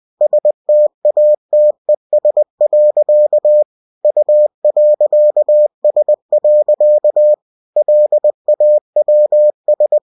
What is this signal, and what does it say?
Morse code, heard as a keyed tone, about 20 wpm, 600 Hz. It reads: STATES. U.S. LAWS